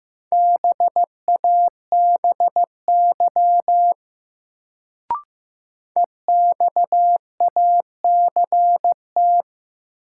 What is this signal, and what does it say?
Morse code, heard as a keyed tone, about 16 wpm, 700 Hz. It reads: BABY EXACT
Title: Sentences from Top 1300 Words - New Word Drill Alt - 15wpm